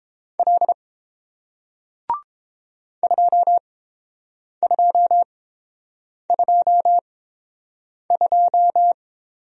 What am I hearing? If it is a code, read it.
Morse code, heard as a keyed tone, about 29 wpm, spaced as for 7 wpm, 700 Hz. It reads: L 2222